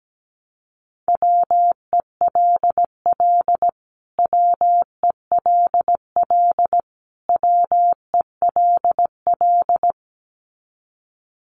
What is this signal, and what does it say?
Morse code, heard as a keyed tone, about 17 wpm, 700 Hz. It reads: WELL WELL WELL
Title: Top 200 Words Repeated 17wpm